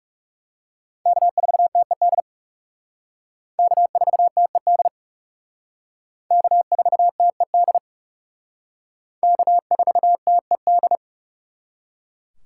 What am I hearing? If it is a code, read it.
Morse code, heard as a keyed tone, about 36 wpm, 700 Hz. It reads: K4TED K4TED K4TED K4TED
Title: Speed Racing - USA 1x3 Call Signs 30wpm